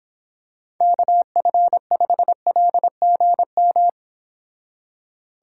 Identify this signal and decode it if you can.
Morse code, heard as a keyed tone, about 25 wpm, 700 Hz. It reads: KF5LGM